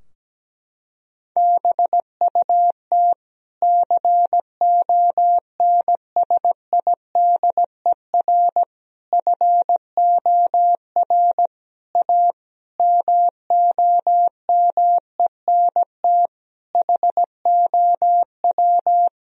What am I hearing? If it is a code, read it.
Morse code, heard as a keyed tone, about 17 wpm, 700 Hz. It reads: BUT CONSIDER FOR A MOMENT HOW
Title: Sentences from Top 1000 Words - Review Part 4 - 17wpm